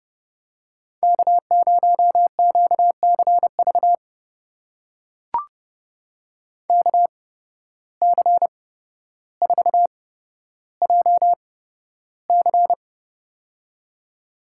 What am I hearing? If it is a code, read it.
Morse code, heard as a keyed tone, about 31 wpm, spaced as for 13 wpm, 700 Hz. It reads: K0QCV KC4JC
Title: USA Call Signs - 5 Characters - 8x Character Spacing - 30wpm